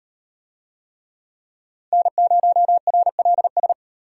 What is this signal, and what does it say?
Morse code, heard as a keyed tone, about 38 wpm, 700 Hz. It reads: N0RLS